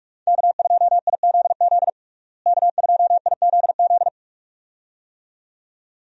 Morse code, heard as a keyed tone, about 46 wpm, 700 Hz. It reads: K2IZZ K2IZZ
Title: USA Call Signs - 5 Characters - Repeated 2x - 45wpm